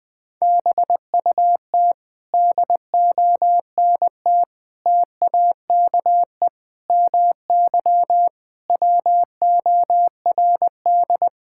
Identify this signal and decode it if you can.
Morse code, heard as a keyed tone, about 20 wpm, 700 Hz. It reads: BUT DONT TAKE MY WORD
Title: Sentences with 7-Word Phrases - 20wpm